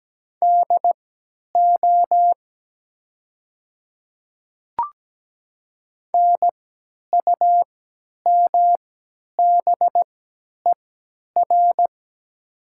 Morse code, heard as a keyed tone, about 17 wpm, spaced as for 10 wpm, 700 Hz. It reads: DO NUMBER
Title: 3x Character Spacing - Top 100 Words - 17wpm